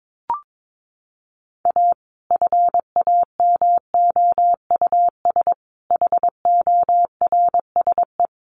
Morse code, heard as a keyed tone, about 22 wpm, 700 Hz. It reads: A FAMOUS HORSE
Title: Sentences from Top 1500 Words 22wpm